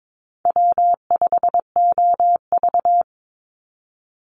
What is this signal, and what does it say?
Morse code, heard as a keyed tone, about 22 wpm, 700 Hz. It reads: W5OV
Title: Easy Call Signs Repeated 3x - 22wpm